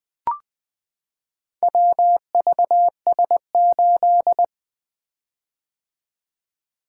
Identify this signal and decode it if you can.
Morse code, heard as a keyed tone, about 21 wpm, 700 Hz. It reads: WVS8
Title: Four Letter-Number Combinations - 20wpm